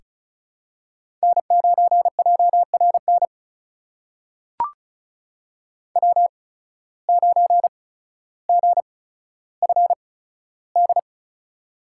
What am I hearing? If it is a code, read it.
Morse code, heard as a keyed tone, about 36 wpm, spaced as for 15 wpm, 700 Hz. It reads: N9JRN W9GFD
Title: USA Call Signs - 5 Characters - 8x Character Spacing - 35wpm